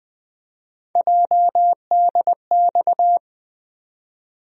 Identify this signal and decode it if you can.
Morse code, heard as a keyed tone, about 20 wpm, 700 Hz. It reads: JDX
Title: USA Call Sign Postfixes - 3 Letters 20wpm